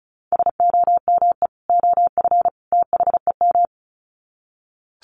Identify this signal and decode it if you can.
Morse code, heard as a keyed tone, about 34 wpm, 700 Hz. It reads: SOME OF THEM
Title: Sets of 3 Words 35wpm